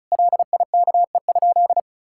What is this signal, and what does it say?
Morse code, heard as a keyed tone, about 36 wpm, 700 Hz. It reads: LIKE?
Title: Sentences from Top 1000 Words - Review Part 6 - 35wpm